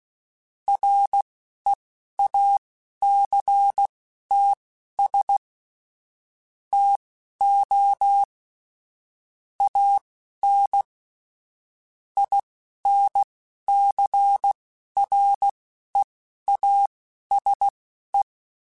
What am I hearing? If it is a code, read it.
Morse code, heard as a keyed tone, about 16 wpm, spaced as for 11 wpm, 800 Hz. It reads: REACTS TO AN INCREASE
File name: Technical Subjects / Technical Subjects-mp3s / XC @10